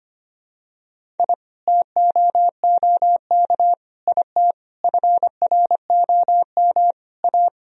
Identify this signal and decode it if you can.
Morse code, heard as a keyed tone, about 25 wpm, 700 Hz. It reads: I TOOK IT FROM A
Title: Sentences from Top 1000 Words - Review Part 1 - 25wpm